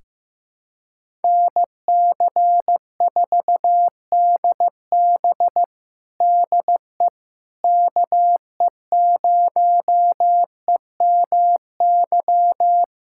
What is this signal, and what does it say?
Morse code, heard as a keyed tone, about 15 wpm, 700 Hz. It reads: NC4DB DE KE0EMY
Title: USA Callsign DE Callsign 15wpm